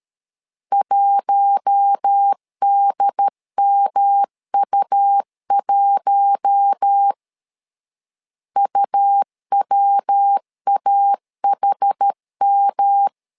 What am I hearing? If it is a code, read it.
Morse code, heard as a keyed tone, about 13 wpm, 800 Hz. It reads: 1DMU1 UWAHM
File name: LEZIONI / 4 Q1 12wpm 2028